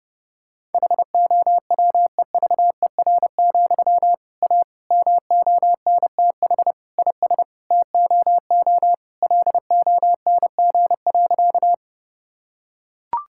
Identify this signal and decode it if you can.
Morse code, heard as a keyed tone, about 30 wpm, 700 Hz. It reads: HOWEVER, A MONTH IS TOO LONG.
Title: Sentences from Top 1000 Words - Review Part 5 - 30wpm